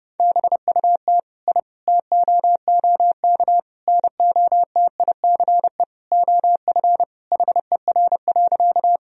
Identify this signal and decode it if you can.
Morse code, heard as a keyed tone, about 32 wpm, 700 Hz. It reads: BUT I TOOK NOTICE OF HER.
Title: Sentences from Top 1000 Words - Review Part 2 - 30wpm